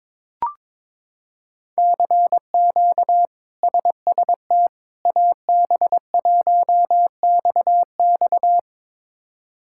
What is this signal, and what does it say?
Morse code, heard as a keyed tone, about 21 wpm, 700 Hz. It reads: CQ SST AB1XX
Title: K1USN SST QSO Practice - 22wpm